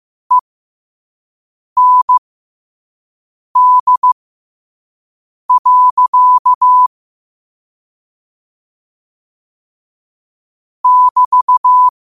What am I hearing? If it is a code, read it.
Morse code, heard as a keyed tone, about 15 wpm, spaced as for 5 wpm, 1000 Hz. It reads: END. <BT>
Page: New quotes every day in morse code at 5 Words per minute.